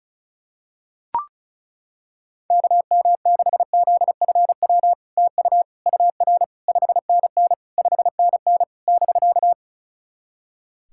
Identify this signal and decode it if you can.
Morse code, heard as a keyed tone, about 36 wpm, 700 Hz. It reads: KM6ZFW TU UR 5NN 5NN <BK>
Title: SOTA QSO Practice 35wpm